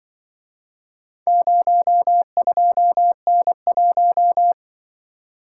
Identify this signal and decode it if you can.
Morse code, heard as a keyed tone, about 25 wpm, 700 Hz. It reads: 02N1